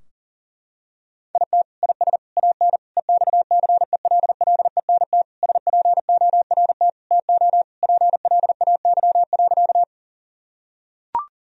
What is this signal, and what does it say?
Morse code, heard as a keyed tone, about 40 wpm, 700 Hz. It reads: IT IS AN EXCELLENT SPORT TO PLAY.
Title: Sentences from Top 1000 Words - Review Part 4 - 40wpm